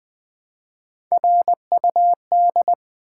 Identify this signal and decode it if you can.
Morse code, heard as a keyed tone, about 20 wpm, 700 Hz. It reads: RUD
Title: Build A Word - 1 Letter At A Time - 4-Letter Words - 20wpm